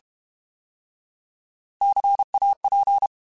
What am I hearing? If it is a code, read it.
Morse code, heard as a keyed tone, about 32 wpm, 800 Hz. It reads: CAP
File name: Alphabetical / Alphabetical mp3s / Cs @30